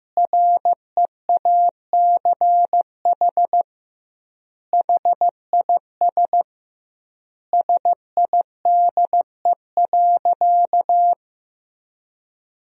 Morse code, heard as a keyed tone, about 15 wpm, 700 Hz. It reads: REACH HIS SIDE.
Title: 2x Word-Spacing - Sentences from Top 400 Words - 15wpm